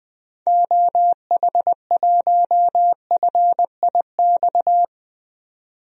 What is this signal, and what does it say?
Morse code, heard as a keyed tone, about 20 wpm, 700 Hz. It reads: OH1FIX
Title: DXCC-100 Call Signs - 6 Characters - 20wpm